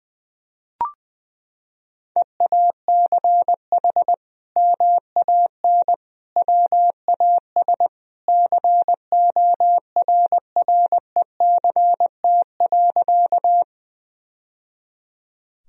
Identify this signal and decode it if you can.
Morse code, heard as a keyed tone, about 20 wpm, 700 Hz. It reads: EACH MAN WAS CORRECT.
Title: Sentences from Top 1000 Words - Review Part 2 - 20wpm